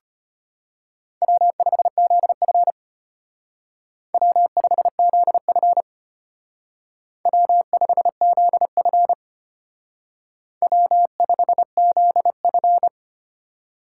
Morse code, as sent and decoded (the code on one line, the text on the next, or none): .-- ..... --.. ..-. / .-- ..... --.. ..-. / .-- ..... --.. ..-. / .-- ..... --.. ..-.
W5ZF W5ZF W5ZF W5ZF